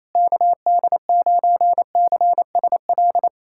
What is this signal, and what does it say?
Morse code, heard as a keyed tone, about 28 wpm, 700 Hz. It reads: KD9CSL
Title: USA Callsign DE Callsign - 6 Characters - 28wpm